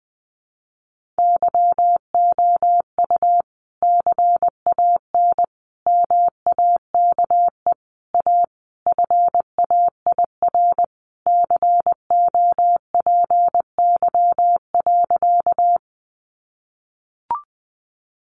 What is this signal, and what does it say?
Morse code, heard as a keyed tone, about 19 wpm, 700 Hz. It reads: YOU CAN MAKE A FAIR COPY.
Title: Sentences from Top 1400 Words 20wpm